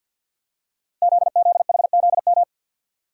Morse code, heard as a keyed tone, about 50 wpm, 700 Hz. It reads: ZZHZM
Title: Code Groups - 5 Letters - 50wpm